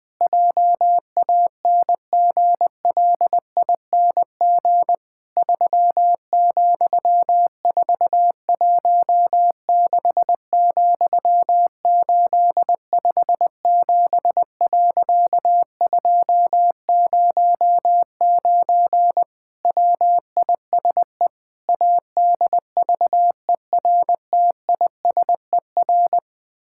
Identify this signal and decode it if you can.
Morse code, heard as a keyed tone, about 20 wpm, 700 Hz. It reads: JANGLING 3,416,857.209 WISE ADVERTISER
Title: Pangram Challenge v2 - 20wpm